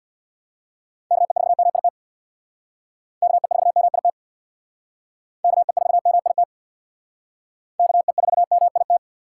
Morse code, heard as a keyed tone, about 60 wpm, 700 Hz. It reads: KE4MIT KE4MIT KE4MIT KE4MIT
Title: Speed Racing - USA 2x3 Call Signs 50wpm